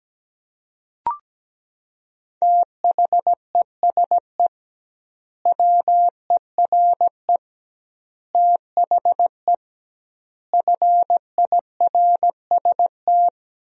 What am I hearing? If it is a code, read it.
Morse code, heard as a keyed tone, about 17 wpm, 700 Hz. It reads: THESE WERE THE FIRST
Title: 2x Word-Spacing - Sentences from Top 300 Words - 17wpm